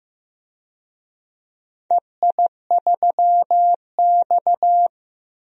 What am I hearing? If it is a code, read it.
Morse code, heard as a keyed tone, about 15 wpm, 700 Hz. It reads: EI3X
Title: DXCC-100 Call Signs - 4 Characters - 15wpm